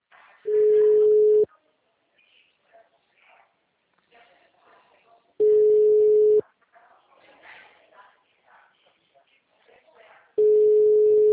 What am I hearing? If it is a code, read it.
Ringback tone (Europe, 425 Hz), with speech.